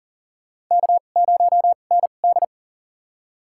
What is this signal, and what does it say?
Morse code, heard as a keyed tone, about 40 wpm, spaced as for 29 wpm, 690 Hz. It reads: K0ND
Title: USA Call Signs - 4 Characters - 2x Character Spacing - 40wpm